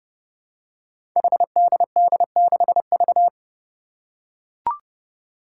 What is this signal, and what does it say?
Morse code, heard as a keyed tone, about 30 wpm, 700 Hz. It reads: HDD6V
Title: Five Challenging Letter-Number Combinations 30wpm